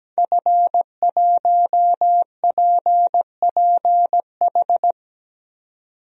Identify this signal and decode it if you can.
Morse code, heard as a keyed tone, about 17 wpm, 700 Hz. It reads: F1PPH
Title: DXCC-100 Call Signs - 5 Characters - 17wpm